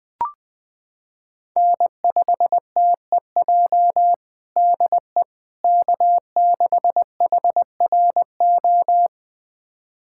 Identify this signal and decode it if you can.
Morse code, heard as a keyed tone, about 20 wpm, 700 Hz. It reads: N5TEJ DE K6HRO
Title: USA Callsign DE Callsign 20wpm